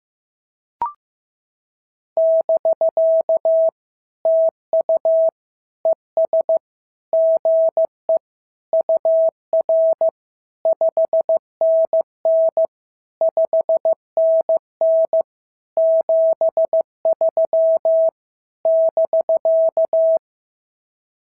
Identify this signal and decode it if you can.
Morse code, heard as a keyed tone, about 14 wpm, 650 Hz. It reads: <BK> TU ES GE UR 5NN 5NN 73 <BK>
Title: SOTA QSO Practice 15wpm